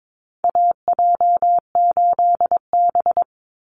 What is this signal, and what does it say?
Morse code, heard as a keyed tone, about 22 wpm, 700 Hz. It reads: AJ8B
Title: Call Signs Repeated 2x - 22wpm